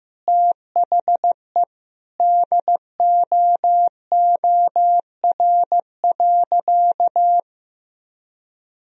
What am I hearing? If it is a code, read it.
Morse code, heard as a keyed tone, about 15 wpm, 700 Hz. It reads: THE DOOR.
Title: Sentences from Top 1000 Words - Review Part 2 - 15wpm